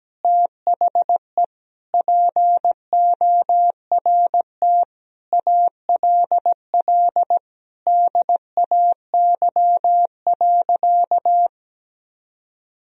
Morse code, heard as a keyed tone, about 17 wpm, 700 Hz. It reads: THE PORT ALL DAY.
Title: Sentences from Top 1000 Words - Review Part 1 - 17wpm